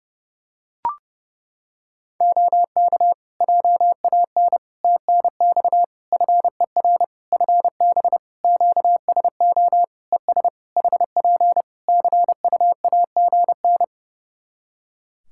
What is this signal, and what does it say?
Morse code, heard as a keyed tone, about 31 wpm, 700 Hz. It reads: OK JAN TNX FER FB QSO ES HP CUAGN
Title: K1USN SST QSO Practice 30wpm